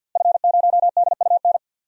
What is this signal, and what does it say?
Morse code, heard as a keyed tone, about 51 wpm, 700 Hz. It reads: F0DUN
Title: DXCC-100 Call Signs - 5 Characters - Repeated 2x - 50wpm